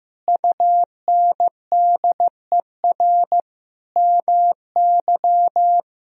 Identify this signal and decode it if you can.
Morse code, heard as a keyed tone, about 15 wpm, 700 Hz. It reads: UNDER MY
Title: Sentences from Top 1000 Words - Review Part 1 - 15wpm